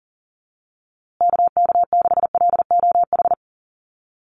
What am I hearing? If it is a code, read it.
Morse code, heard as a keyed tone, about 40 wpm, 700 Hz. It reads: KK6LOH
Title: USA Call Signs Repeated 2x 40wpm